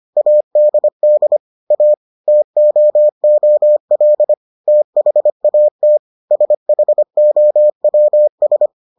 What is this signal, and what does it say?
Morse code, heard as a keyed tone, about 25 wpm, 600 Hz. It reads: ADD A TOOL THAT SHOWS